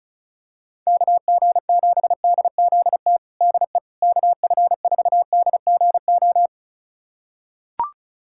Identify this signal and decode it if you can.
Morse code, heard as a keyed tone, about 36 wpm, 700 Hz. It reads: KG7DZT DE KF4DGO
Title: USA Callsign DE Callsign - 6 Characters - 35wpm